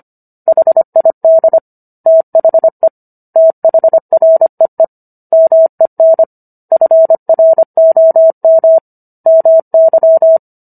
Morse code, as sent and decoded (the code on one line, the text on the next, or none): .... .. -.. / - .... / - .... .-. . / -- . -. / ..-. .-. --- -- / -- -.--
HID THE THREE MEN FROM MY